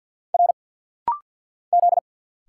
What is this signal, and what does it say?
Morse code, heard as a keyed tone, about 50 wpm, 700 Hz. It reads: R Z
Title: Single Letter-Number - Rapid-Fire 50wpm